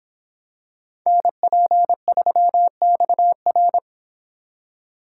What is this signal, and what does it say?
Morse code, heard as a keyed tone, about 26 wpm, 700 Hz. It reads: NP3XR